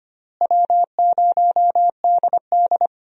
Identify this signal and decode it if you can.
Morse code, heard as a keyed tone, about 25 wpm, 700 Hz. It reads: W0DD
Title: USA Call Signs - 4 Characters - Repeated 2x - 25wpm